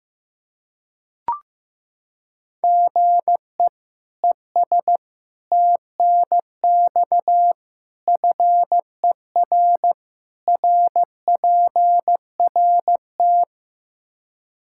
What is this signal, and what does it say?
Morse code, heard as a keyed tone, about 15 wpm, 700 Hz. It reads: GE ES TNX FER RPRT
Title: K1USN SST QSO Practice 15wpm